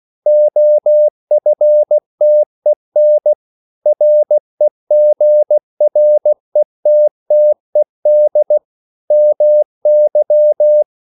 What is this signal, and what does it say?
Morse code, heard as a keyed tone, about 16 wpm, 600 Hz. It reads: OFTEN REGRETTED MY
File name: CW / Practice / EnglishQuotations0216